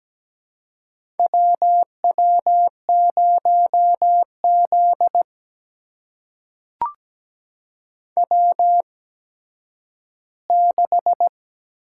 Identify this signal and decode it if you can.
Morse code, heard as a keyed tone, about 17 wpm, spaced as for 10 wpm, 700 Hz. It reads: WW0Z W6
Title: USA Call Signs - 4 Characters - 8x Character Spacing - 17wpm